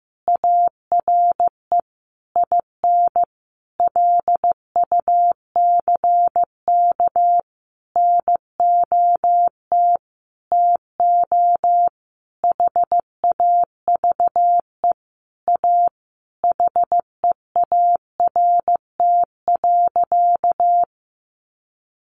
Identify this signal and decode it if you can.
Morse code, heard as a keyed tone, about 15 wpm, 700 Hz. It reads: ARE IN LUCK NOT TO HAVE A HEART.
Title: Sentences from Top 1500 Words 15wpm